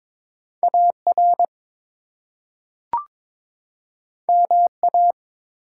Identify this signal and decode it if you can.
Morse code, heard as a keyed tone, about 22 wpm, 700 Hz. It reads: AR MA
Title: US States and Canadian Provinces 22wpm